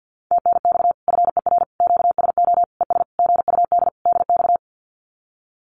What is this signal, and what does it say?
Morse code, heard as a keyed tone, about 48 wpm, 700 Hz. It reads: TNX FER QSO ES GUD DX
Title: Common QSO Phrases Encore 50wpm